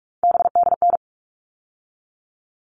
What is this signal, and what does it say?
Morse code, heard as a keyed tone, about 44 wpm, 700 Hz. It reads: BDN
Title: All Cut Numbers - 3-Digit 45wpm